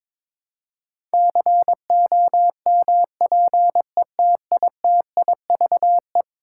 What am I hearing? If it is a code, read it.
Morse code, heard as a keyed tone, about 22 wpm, 700 Hz. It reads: COMPETITIVE
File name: practice / 22wpm / Eleven Letter Words - 22wpm